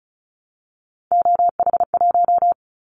Morse code, heard as a keyed tone, about 34 wpm, 700 Hz. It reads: OH1
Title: Callsign Prefixes for Top 25 DXCC - Repeated 3x - 35wpm